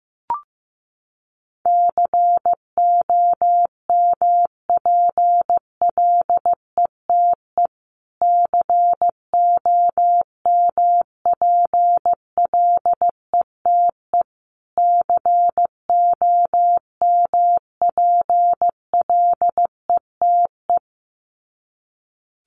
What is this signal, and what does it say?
Morse code, heard as a keyed tone, about 15 wpm, 700 Hz. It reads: COMPLETE COMPLETE COMPLETE
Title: Top 500 Words Repeated 15wpm